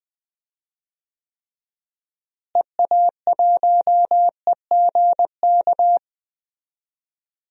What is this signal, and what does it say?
Morse code, heard as a keyed tone, about 20 wpm, 700 Hz. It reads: EA1EGK
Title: DXCC-100 Call Signs - 6 Characters - Repeated 2x - 20wpm